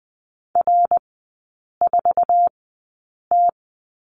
Morse code, heard as a keyed tone, about 20 wpm, spaced as for 8 wpm, 690 Hz. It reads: R4T
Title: 093 Lesson 16e - Letter c - Three Characters 20wpm